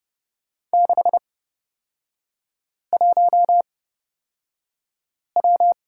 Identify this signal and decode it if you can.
Morse code, heard as a keyed tone, about 29 wpm, spaced as for 5 wpm, 690 Hz. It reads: 61W
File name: Lesson 38e - Letter q - Three Characters 30wpm